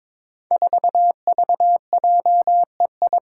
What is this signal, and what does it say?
Morse code, heard as a keyed tone, about 22 wpm, 700 Hz. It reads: 4VJEI
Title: Five Challenging Letter-Number Combinations 22wpm